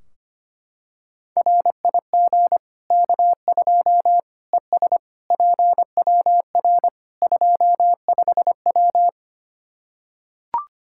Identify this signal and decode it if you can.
Morse code, heard as a keyed tone, about 25 wpm, 700 Hz. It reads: RIG K2 ES PWR 25W
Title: Practice QSO - Exchange 2 - Info - 25wpm